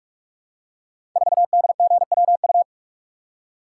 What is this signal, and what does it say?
Morse code, heard as a keyed tone, about 44 wpm, 700 Hz. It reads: 4DGWU
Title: Code Groups - 5 Letter:Numbers - 45wpm